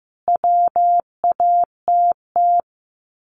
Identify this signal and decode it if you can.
Morse code, heard as a keyed tone, about 14 wpm, 700 Hz. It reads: WATT
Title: Common QSO Phrases 15wpm